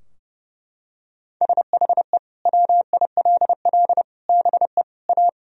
Morse code, heard as a keyed tone, about 30 wpm, 700 Hz. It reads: SHE WILL BE A
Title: Sentences from Top 1000 Words - Review Part 4 - 30wpm